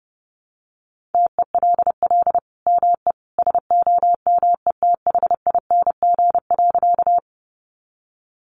Morse code, heard as a keyed tone, about 30 wpm, 700 Hz. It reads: TELL ME SOMETHING.
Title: Sentences from Top 600 Words 30wpm